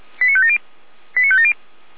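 Five-tone selective call (ZVEI), heard as quick runs of five tones, tones of about 70 ms. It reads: 89570 89570
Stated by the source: Received by SDR, associated with feuerwehr-orlamuende.org (fire brigade Orlamünde).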